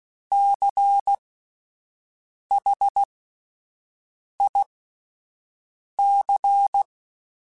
Morse code, heard as a keyed tone, about 16 wpm, spaced as for 5 wpm, 800 Hz. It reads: CHIC
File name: Alphabetical / Alphabetical mp3s / Cs @5